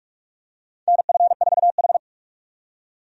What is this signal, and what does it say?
Morse code, heard as a keyed tone, about 46 wpm, 700 Hz. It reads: NF4H
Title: USA Call Signs - 4 Characters - 45wpm